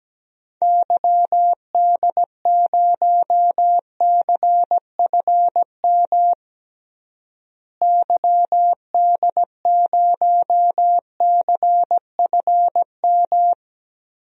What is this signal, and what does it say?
Morse code, heard as a keyed tone, about 17 wpm, 700 Hz. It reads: YD0CFM YD0CFM
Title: DXCC-100 Call Signs - 6 Characters - Repeated 3x - 17wpm